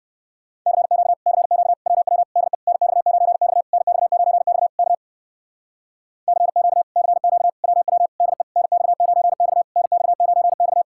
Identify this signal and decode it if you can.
Morse code, heard as a keyed tone, about 56 wpm, 700 Hz. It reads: CQ CQ WW DE NX9X NX9X K CQ CQ WW DE NX9X NX9X